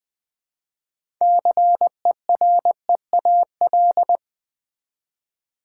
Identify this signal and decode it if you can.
Morse code, heard as a keyed tone, about 20 wpm, 690 Hz. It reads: CEREAL
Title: Easy Words with Hints - 20wpm